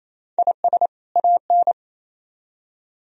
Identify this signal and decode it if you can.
Morse code, heard as a keyed tone, about 37 wpm, 700 Hz. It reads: IS AN